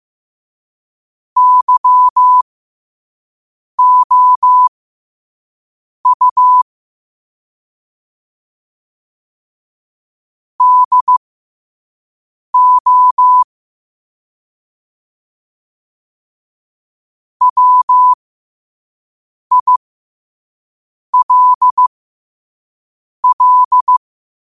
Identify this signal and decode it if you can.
Morse code, heard as a keyed tone, about 15 wpm, spaced as for 5 wpm, 1000 Hz. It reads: YOU DO WILL
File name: qotdfiles / 5wpm